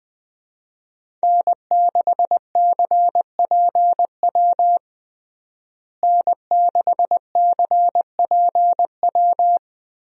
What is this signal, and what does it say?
Morse code, heard as a keyed tone, about 20 wpm, 700 Hz. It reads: N6CPW N6CPW
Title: USA Call Signs - 5 Characters - Repeated 2x - 20wpm